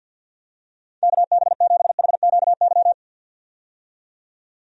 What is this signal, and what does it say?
Morse code, heard as a keyed tone, about 50 wpm, 700 Hz. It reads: KB7HQY
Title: USA Call Signs - 6 Characters - 50wpm